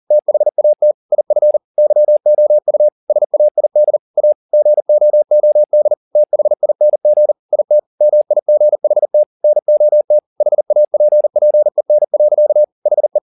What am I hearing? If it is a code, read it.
Morse code, heard as a keyed tone, about 40 wpm, 600 Hz. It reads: THAT IF YOU SAID A GOOD THING IT MIGHT NOT HAPPEN. HE